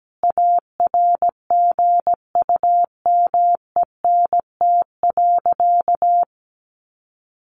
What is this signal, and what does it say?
Morse code, heard as a keyed tone, about 17 wpm, 700 Hz. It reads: ARGUMENT.
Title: Sentences from Top 1400 Words 17wpm